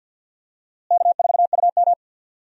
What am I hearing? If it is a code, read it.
Morse code, heard as a keyed tone, about 53 wpm, 700 Hz. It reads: K4UM
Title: USA Call Signs - 4 Characters - Repeated 3x - 50wpm